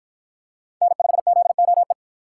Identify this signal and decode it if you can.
Morse code, heard as a keyed tone, about 48 wpm, 700 Hz. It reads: N5ZOE